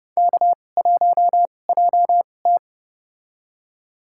Morse code, heard as a keyed tone, about 30 wpm, spaced as for 22 wpm, 700 Hz. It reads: K1JT
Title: USA Call Signs - 4 Characters - 2x Character Spacing - 30wpm